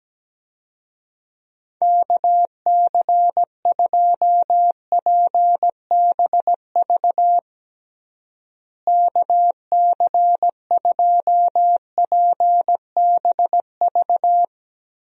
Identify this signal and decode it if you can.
Morse code, heard as a keyed tone, about 17 wpm, 700 Hz. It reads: KC2PBV KC2PBV
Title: USA Call Signs - 6 Characters - Repeated 3x - 17wpm